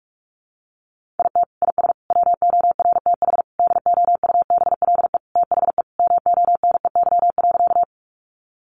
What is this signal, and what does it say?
Morse code, heard as a keyed tone, about 44 wpm, 700 Hz. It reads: IT IS WORTH DOUBLE THE MONEY.
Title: Sentences from Top 1400 Words 45wpm